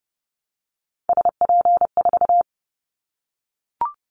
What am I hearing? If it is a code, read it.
Morse code, heard as a keyed tone, about 30 wpm, 700 Hz. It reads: SP4
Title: Callsign Prefixes for Top 25 DXCC - Repeated 3x - 30wpm